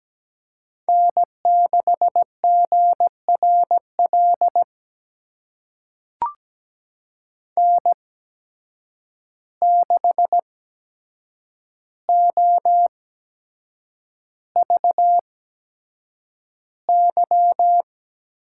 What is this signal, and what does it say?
Morse code, heard as a keyed tone, about 17 wpm, spaced as for 7 wpm, 700 Hz. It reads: N6GRL N6OVY
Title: USA Call Signs - 8x Character Spacing - 17wpm